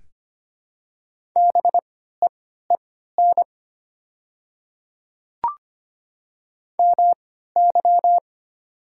Morse code, heard as a keyed tone, about 25 wpm, spaced as for 14 wpm, 700 Hz. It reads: BEEN MY
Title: 3x Character Spacing - Top 100 Words - 25wpm